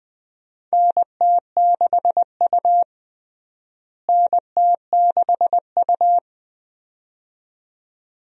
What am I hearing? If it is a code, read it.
Morse code, heard as a keyed tone, about 20 wpm, 700 Hz. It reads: NT6U NT6U
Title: USA Call Signs - 4 Characters - Repeated 2x - 20wpm